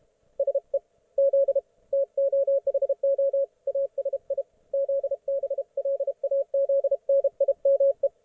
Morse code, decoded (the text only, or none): SE Z TOHO ASI ZBLAZNIME